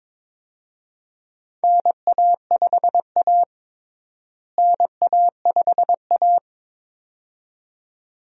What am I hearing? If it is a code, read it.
Morse code, heard as a keyed tone, about 22 wpm, 700 Hz. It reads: NA5A NA5A
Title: USA Call Signs - 4 Characters - Repeated 2x - 22wpm